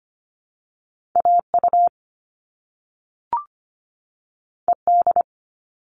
Morse code, heard as a keyed tone, about 25 wpm, 700 Hz. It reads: AU ED
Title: All Cut Numbers - 2-Digit 25wpm